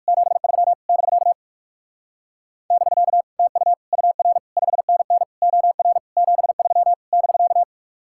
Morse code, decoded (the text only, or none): 73 <BK> <BK> TU UR 5NN OR 73 <BK>